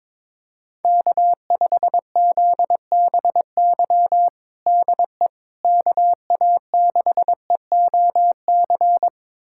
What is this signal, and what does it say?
Morse code, heard as a keyed tone, about 22 wpm, 700 Hz. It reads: K5ZBY DE KA6EOC
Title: USA Callsign DE Callsign 22wpm